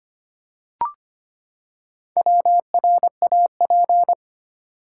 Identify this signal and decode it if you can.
Morse code, heard as a keyed tone, about 25 wpm, 700 Hz. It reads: WRAP
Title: Four Letter Words - 25wpm